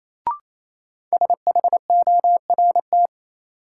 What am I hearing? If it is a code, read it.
Morse code, heard as a keyed tone, about 28 wpm, 700 Hz. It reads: SHORT
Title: Sets of 5 Words Encore - New Word Drill - 28wpm